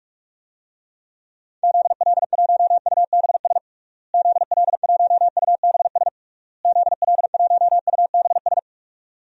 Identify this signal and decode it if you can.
Morse code, heard as a keyed tone, about 46 wpm, 700 Hz. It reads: ZL1UBS ZL1UBS ZL1UBS
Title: DXCC-100 Call Signs - 6 Characters - Repeated 3x - 45wpm